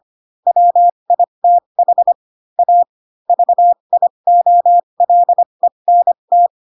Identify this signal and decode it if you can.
Morse code, heard as a keyed tone, about 25 wpm, spaced as for 22 wpm, 700 Hz. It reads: WITH A VIOLENT
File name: Pinocchio / Pinocchio ch2 18 25 ik1hge